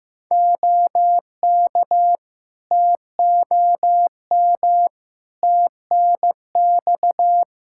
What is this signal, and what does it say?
Morse code, heard as a keyed tone, about 15 wpm, 700 Hz. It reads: OK TOM TNX
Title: Practice QSO - Exchange 4 - Ending - 15wpm